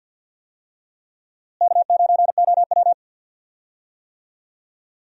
Morse code, heard as a keyed tone, about 52 wpm, 700 Hz. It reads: K9OW
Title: USA Call Signs - 4 Characters - Repeated 3x - 50wpm